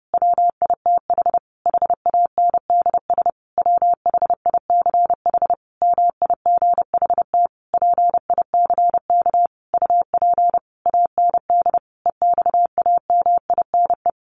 Morse code, heard as a keyed tone, about 30 wpm, 700 Hz. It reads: WITH HANDS WHICH MIGHT PICK UP AND EXAMINE
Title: A Sound Of Thunder 30wpm